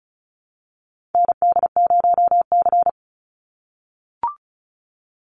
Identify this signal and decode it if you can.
Morse code, heard as a keyed tone, about 34 wpm, 700 Hz. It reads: ND0C
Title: NA SOTA Chaser Callsigns 35wpm